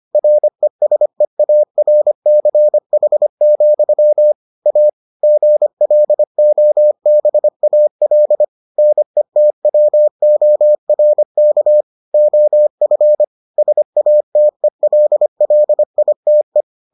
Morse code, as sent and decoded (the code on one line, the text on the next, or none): .-. ... .- .-. -.-. .... --..-- / .- / --. .-.. --- -... .- .-.. / -. - .-- --- .-. -.- / --- ..-. / ... .- - . .-.. .-.. .. - .
RESEARCH, A GLOBAL NETWORK OF SATELLITE